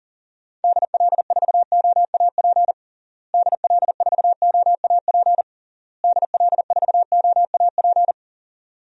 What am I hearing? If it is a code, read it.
Morse code, heard as a keyed tone, about 40 wpm, 700 Hz. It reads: DL4OAP DL4OAP DL4OAP
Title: USA Call Signs - 6 Characters - Repeated 3x - 40wpm